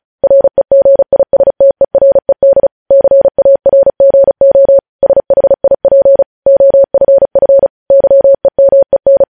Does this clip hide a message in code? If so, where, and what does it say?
Yes: Morse code from 0.2 to 9.2 s, heard as a keyed tone, about 36 wpm, 550 Hz. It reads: REGISTERED CARGO SHIP OFF YEMEN